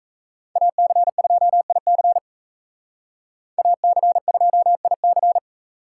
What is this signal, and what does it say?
Morse code, heard as a keyed tone, about 35 wpm, 700 Hz. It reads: AC2IC AC2IC